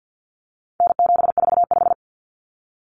Morse code, heard as a keyed tone, about 50 wpm, 700 Hz. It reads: N745
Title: Common Cut Numbers - 4-Digit 50wpm